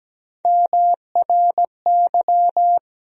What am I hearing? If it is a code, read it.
Morse code, heard as a keyed tone, about 17 wpm, 700 Hz. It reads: MRY
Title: USA Call Sign Postfixes - 3 Letters 17wpm